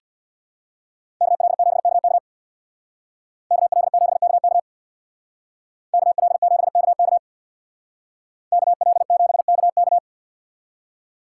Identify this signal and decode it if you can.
Morse code, heard as a keyed tone, about 64 wpm, 700 Hz. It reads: KL7KK KL7KK KL7KK KL7KK